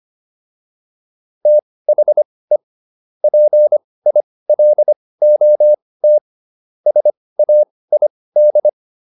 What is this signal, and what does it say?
Morse code, heard as a keyed tone, about 25 wpm, spaced as for 18 wpm, 600 Hz. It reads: THE PILOT SAID